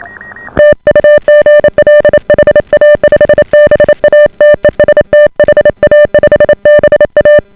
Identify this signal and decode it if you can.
Morse code, heard as a keyed tone, about 28 wpm, 590 Hz. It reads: TU GL HA5BA TEST HA5BA